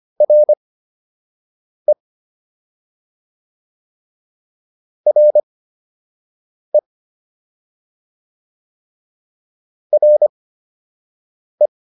Morse code, heard as a keyed tone, about 25 wpm, spaced as for 6 wpm, 600 Hz. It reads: RE RE RE